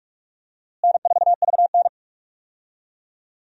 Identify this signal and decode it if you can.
Morse code, heard as a keyed tone, about 43 wpm, 700 Hz. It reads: N4VN
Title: USA Call Signs - 4 Characters - Repeated 2x - 45wpm